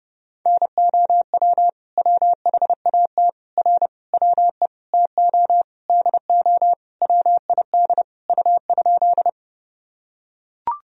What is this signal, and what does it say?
Morse code, heard as a keyed tone, about 30 wpm, 700 Hz. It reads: NOW WHAT R WE TO DO WID U?
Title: Sentences from Top 100 Words and Abbreviations 30wpm